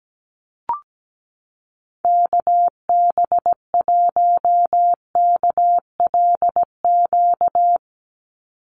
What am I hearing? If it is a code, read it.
Morse code, heard as a keyed tone, about 17 wpm, 700 Hz. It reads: KB1KLQ
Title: USA Call Signs 17wpm